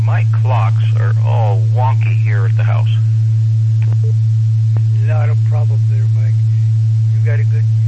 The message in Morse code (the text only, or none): none